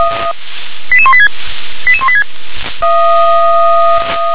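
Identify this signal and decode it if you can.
Five-tone selective call (ZVEI), heard as quick runs of five tones, tones of about 70 ms. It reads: 88186 88186
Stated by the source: ZVEI five-tone selective call received by SDR.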